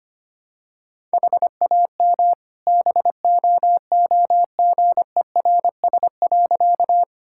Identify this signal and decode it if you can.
Morse code, heard as a keyed tone, about 25 wpm, 700 Hz. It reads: HAM BOOGERS.
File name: practice / 25wpm / All-Age Appropriate Jokes - 25wpm